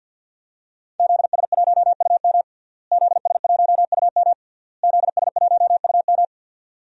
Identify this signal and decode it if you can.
Morse code, heard as a keyed tone, about 48 wpm, 700 Hz. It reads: ZS1UM ZS1UM ZS1UM
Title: DXCC-100 Call Signs - 5 Characters - Repeated 3x - 50wpm